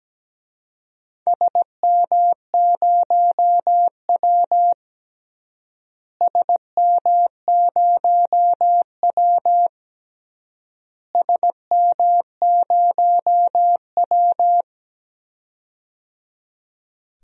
Morse code, heard as a keyed tone, about 17 wpm, 700 Hz. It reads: SM0W SM0W SM0W
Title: DXCC-100 Call Signs - 4 Characters - Repeated 3x - 17wpm